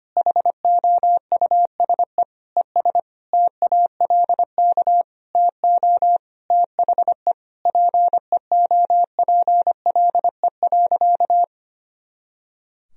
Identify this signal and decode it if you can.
Morse code, heard as a keyed tone, about 25 wpm, 700 Hz. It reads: HOUSE ES TALK TO THE PEOPLE.
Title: Sentences from Top 400 Words and Abbreviations 25wpm